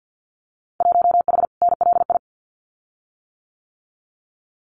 Morse code, heard as a keyed tone, about 50 wpm, 700 Hz. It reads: 1H NLI
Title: Winter Field Day Exchanges 50wpm